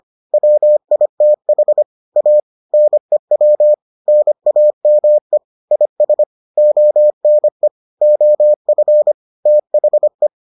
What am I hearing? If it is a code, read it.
Morse code, heard as a keyed tone, about 25 wpm, 600 Hz. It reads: WITH A NEW NAME IS ONE OF THE